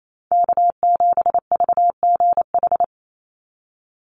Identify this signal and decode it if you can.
Morse code, heard as a keyed tone, about 28 wpm, 700 Hz. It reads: K7VGH
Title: USA Call Signs 28wpm